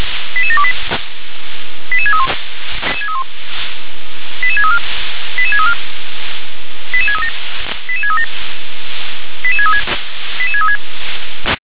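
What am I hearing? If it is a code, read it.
Five-tone selective call (ZVEI), heard as quick runs of five tones, tones of about 70 ms. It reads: 88728 88631 88634 88634 88637 88637 88636 88636
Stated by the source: ZVEI five-tone selective call received by SDR.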